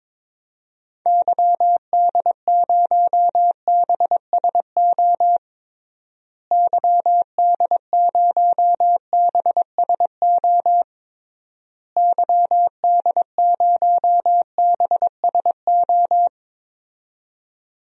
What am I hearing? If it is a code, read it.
Morse code, heard as a keyed tone, about 22 wpm, 700 Hz. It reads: YD0BSO YD0BSO YD0BSO
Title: DXCC-100 Call Signs - 6 Characters - Repeated 3x - 22wpm